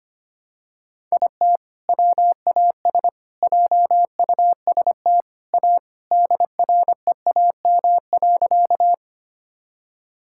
Morse code, heard as a keyed tone, about 25 wpm, 700 Hz. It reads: IT WAS JUST A DREAM.
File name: practice / 25wpm / Sentences with 3-Word Phrases - 25wpm